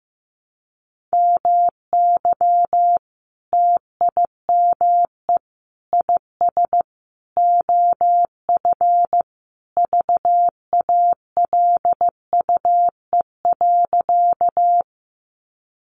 Morse code, heard as a keyed tone, about 15 wpm, 700 Hz. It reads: MY TIME IS OF VALUE.
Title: Sentences from Top 600 Words 15wpm